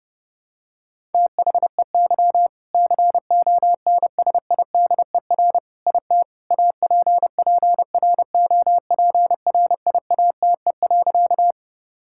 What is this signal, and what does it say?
Morse code, heard as a keyed tone, about 32 wpm, 700 Hz. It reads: THEY CONSIDER IT APPROPRIATE.
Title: Sentences from Top 1000 Words - Review Part 5 - 30wpm